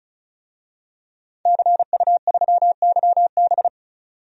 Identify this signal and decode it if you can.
Morse code, heard as a keyed tone, about 34 wpm, 700 Hz. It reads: CU3YB
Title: DXCC-100 Call Signs - 5 Characters - 35wpm